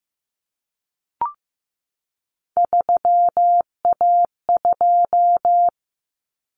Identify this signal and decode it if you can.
Morse code, heard as a keyed tone, about 15 wpm, 700 Hz. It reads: 3A2
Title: Common Cut Numbers - 3-Digit 15wpm